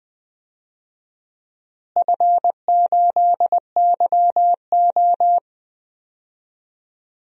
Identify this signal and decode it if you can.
Morse code, heard as a keyed tone, about 20 wpm, 700 Hz. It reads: F8YO
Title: DXCC-100 Call Signs - 4 Characters - Repeated 2x - 20wpm